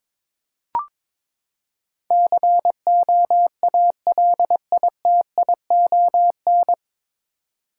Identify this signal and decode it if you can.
Morse code, heard as a keyed tone, about 22 wpm, 700 Hz. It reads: COALITION
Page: Morse Code Ninja practice 22wpm